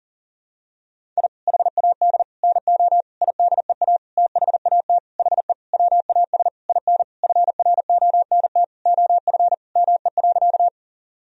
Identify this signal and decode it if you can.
Morse code, heard as a keyed tone, about 40 wpm, 700 Hz. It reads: I HAD NO IDEA THAT HE WAS IN FRONT OF ME.
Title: Sentences from Top 1000 Words - Review Part 2 - 40wpm